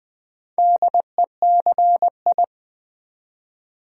Morse code, heard as a keyed tone, about 20 wpm, 700 Hz. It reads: DECI
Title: Build A Word - 2 Letters At A Time - 6-Letter Words - 20wpm